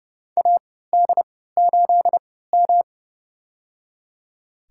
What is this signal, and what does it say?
Morse code, heard as a keyed tone, about 29 wpm, spaced as for 17 wpm, 700 Hz. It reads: AD8M
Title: USA Call Signs - 4 Characters - 3x Character Spacing - 30wpm